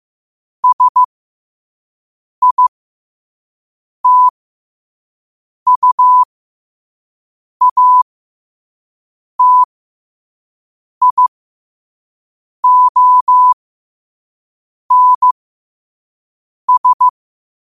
Morse code, heard as a keyed tone, about 15 wpm, spaced as for 5 wpm, 1000 Hz. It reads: SITUATIONS
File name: qotdfiles / 5wpm